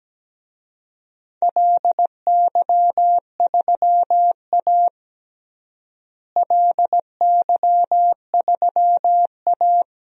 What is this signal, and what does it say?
Morse code, heard as a keyed tone, about 17 wpm, 700 Hz. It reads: LY3A LY3A
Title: DXCC-100 Call Signs - 4 Characters - Repeated 2x - 17wpm